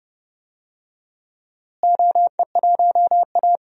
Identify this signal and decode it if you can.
Morse code, heard as a keyed tone, about 30 wpm, 700 Hz. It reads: OE1A
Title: DXCC-100 Call Signs - 4 Characters - Repeated 2x - 30wpm